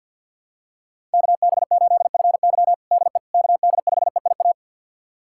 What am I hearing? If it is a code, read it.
Morse code, heard as a keyed tone, about 50 wpm, 700 Hz. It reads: KB8FY DE KD5EIA
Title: USA Callsign DE Callsign 50wpm